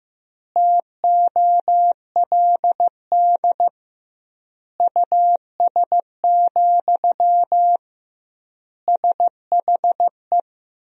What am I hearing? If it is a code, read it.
Morse code, heard as a keyed tone, about 15 wpm, 700 Hz. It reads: TOLD US, SHE
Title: 2x Word-Spacing - Sentences from Top 400 Words - 15wpm